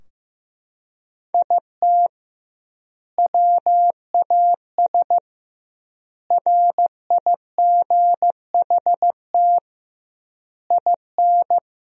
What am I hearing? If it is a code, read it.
Morse code, heard as a keyed tone, about 15 wpm, 700 Hz. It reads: IT WAS RIGHT IN
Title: 2x Word-Spacing - Sentences from Top 500 Words - 15wpm